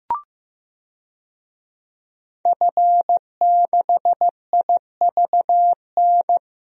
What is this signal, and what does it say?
Morse code, heard as a keyed tone, about 16 wpm, 700 Hz. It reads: F6IVN
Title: DXCC-100 Call Signs - 5 Characters - Repeated 3x - 15wpm